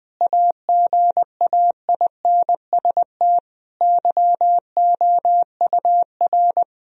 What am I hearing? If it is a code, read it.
Morse code, heard as a keyed tone, about 20 wpm, 700 Hz. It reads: AGAINST YOUR
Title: Sentences from Top 1000 Words - Review Part 3 - 20wpm